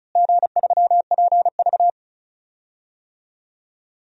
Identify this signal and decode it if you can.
Morse code, heard as a keyed tone, about 34 wpm, 700 Hz. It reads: G3PV
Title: DXCC-100 Call Signs - 4 Characters - Repeated 3x - 35wpm